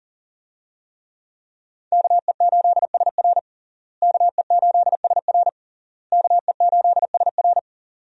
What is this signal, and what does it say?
Morse code, heard as a keyed tone, about 40 wpm, 700 Hz. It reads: KE8SR KE8SR KE8SR
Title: USA Call Signs - 5 Characters - Repeated 3x - 40wpm